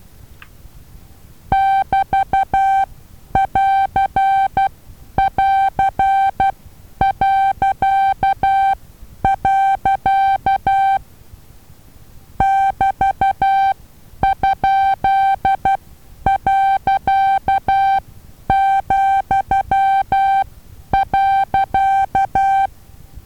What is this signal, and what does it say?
Morse code, heard as a keyed tone, about 12 wpm, spaced as for 9 wpm, 780 Hz. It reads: <BT><AR><AR>.. <BT>?.,.
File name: Taviro HA6VA B